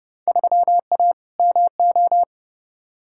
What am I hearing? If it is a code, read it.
Morse code, heard as a keyed tone, about 30 wpm, 700 Hz. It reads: 3A MO
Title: Field Day Exchanges - 30wpm